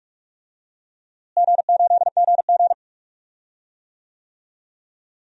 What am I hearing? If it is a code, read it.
Morse code, heard as a keyed tone, about 46 wpm, 700 Hz. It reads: G8GG
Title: DXCC-100 Call Signs - 4 Characters - Repeated 2x - 45wpm